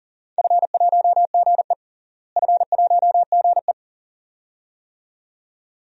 Morse code, heard as a keyed tone, about 40 wpm, 700 Hz. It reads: F1GE F1GE
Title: DXCC-100 Call Signs - 4 Characters - Repeated 2x - 40wpm